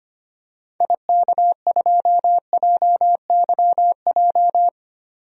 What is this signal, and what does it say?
Morse code, heard as a keyed tone, about 25 wpm, 700 Hz. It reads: IK2JYJ
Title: DXCC-100 Call Signs - 6 Characters - 25wpm